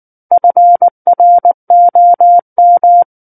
Morse code, heard as a keyed tone, about 19 wpm, 700 Hz. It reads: FROM
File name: Book One / 10-Chapter 10 - 19 WPM